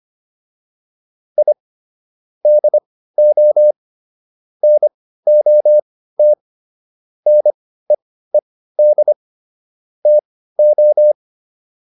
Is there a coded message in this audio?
Yes: Morse code from 1.4 to 11.1 s, heard as a keyed tone, about 25 wpm, spaced as for 15 wpm, 600 Hz. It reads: I DO NOT NEED TO